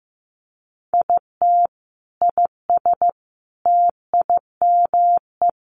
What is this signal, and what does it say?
Morse code, heard as a keyed tone, about 15 wpm, 700 Hz. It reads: IT IS TIME
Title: Sentences from Top 500 Words 15wpm